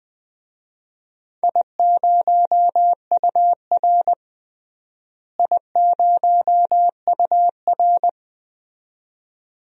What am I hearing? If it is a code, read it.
Morse code, heard as a keyed tone, about 20 wpm, 700 Hz. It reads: I0UR I0UR
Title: DXCC-100 Call Signs - 4 Characters - Repeated 2x - 20wpm